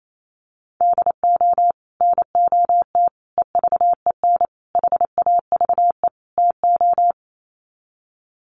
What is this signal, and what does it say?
Morse code, heard as a keyed tone, about 28 wpm, 700 Hz. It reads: DO NOT EVEN HAVE TO
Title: Sets of 5 Words Encore2 28wpm